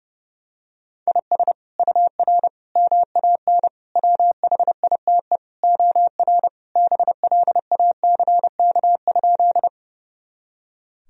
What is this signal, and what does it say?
Morse code, heard as a keyed tone, about 32 wpm, 700 Hz. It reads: IS UR MAN WHITE OR BLACK?
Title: Sentences from Top 400 Words and Abbreviations 30wpm